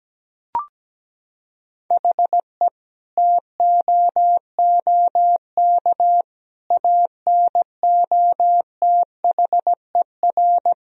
Morse code, heard as a keyed tone, about 17 wpm, 700 Hz. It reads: HE TOOK ANOTHER
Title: Sentences from Top 1000 Words - Review Part 2 - 17wpm